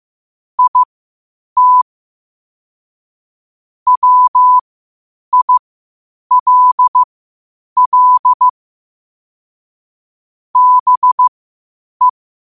Morse code, heard as a keyed tone, about 15 wpm, spaced as for 8 wpm, 1000 Hz. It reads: IT WILL BE
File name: qotdfiles / 8wpm